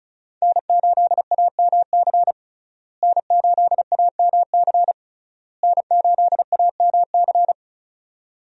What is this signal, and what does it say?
Morse code, heard as a keyed tone, about 36 wpm, 700 Hz. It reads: N8AMC N8AMC N8AMC
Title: USA Call Signs - 5 Characters - Repeated 3x - 35wpm